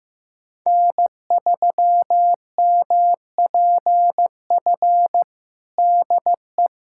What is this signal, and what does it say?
Morse code, heard as a keyed tone, about 16 wpm, 700 Hz. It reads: N3MPF DE
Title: USA Callsign DE Callsign 15wpm